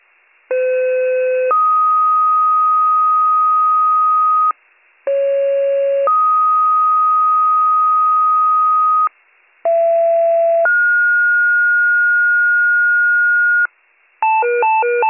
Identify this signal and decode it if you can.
Two-tone page, heard as a one-second tone, then a three-second tone, about 510 then 1250 Hz, about 550 then 1250 Hz, about 670 then 1430 Hz. It is heard from 0.5 to 13.7 s.